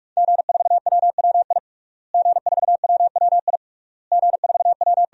Morse code, heard as a keyed tone, about 46 wpm, 700 Hz. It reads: G4WWI G4WWI G4W
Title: DXCC-100 Call Signs - 5 Characters - Repeated 3x - 45wpm